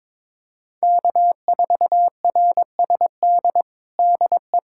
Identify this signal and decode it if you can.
Morse code, heard as a keyed tone, about 22 wpm, 700 Hz. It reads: K4RSD DE
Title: USA Callsign DE Callsign - 5 Characters - 22wpm